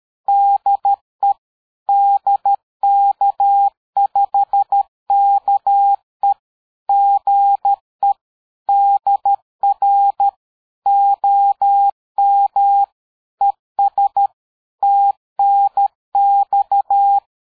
Morse code, heard as a keyed tone, about 13 wpm, 800 Hz. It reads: DE DK5KE GE DR OM ES TNX
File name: dk5ke / QSO 60BpM